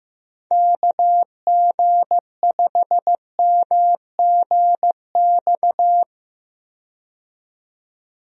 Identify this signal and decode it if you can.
Morse code, heard as a keyed tone, about 15 wpm, 700 Hz. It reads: KG5MGX
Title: USA Call Signs - 6 Characters - 15wpm